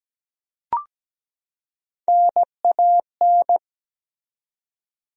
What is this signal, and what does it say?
Morse code, heard as a keyed tone, about 17 wpm, 700 Hz. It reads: NAN
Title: USA Common Names - Max 3 Letters - 17wpm